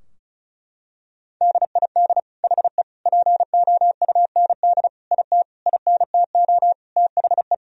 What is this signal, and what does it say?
Morse code, heard as a keyed tone, about 36 wpm, 700 Hz. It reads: DID HE POUND IT INTO THE
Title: Sentences from Top 1000 Words - Review Part 2 - 35wpm